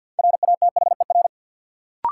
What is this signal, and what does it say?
Morse code, heard as a keyed tone, about 50 wpm, 700 Hz. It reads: RATHER
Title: Sets of 2 Words - New Word Drill - 50wpm